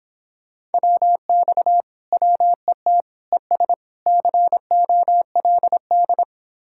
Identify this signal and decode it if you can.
Morse code, heard as a keyed tone, about 31 wpm, 700 Hz. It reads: WX WET ES COLD